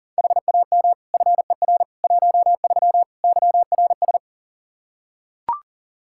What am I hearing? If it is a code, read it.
Morse code, heard as a keyed tone, about 40 wpm, 700 Hz. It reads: HAM FER 13 YRS
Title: Practice QSO - Exchange 3 - More Info - 40wpm